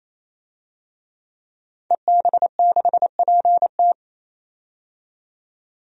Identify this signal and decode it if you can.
Morse code, heard as a keyed tone, about 28 wpm, 700 Hz. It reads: EB6PT
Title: DXCC-100 Call Signs - 5 Characters - 28wpm